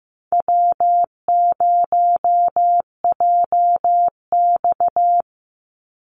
Morse code, heard as a keyed tone, about 15 wpm, 700 Hz. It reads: W0JX
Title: Easy Call Signs Repeated 3x - 15wpm